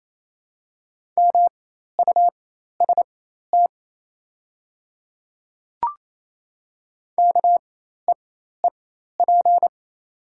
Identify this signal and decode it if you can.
Morse code, heard as a keyed tone, about 28 wpm, spaced as for 13 wpm, 700 Hz. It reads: MUST KEEP
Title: 4x Character Spacing - Top 300 Words - 28wpm